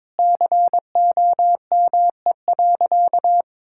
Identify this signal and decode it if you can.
Morse code, heard as a keyed tone, about 22 wpm, 700 Hz. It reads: COME.
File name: Sentences with 7-Word Phrases - 22wpm